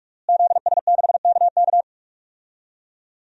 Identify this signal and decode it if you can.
Morse code, heard as a keyed tone, about 44 wpm, 700 Hz. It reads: ZS6KK
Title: DXCC-100 Call Signs - 5 Characters - Repeated 3x - 45wpm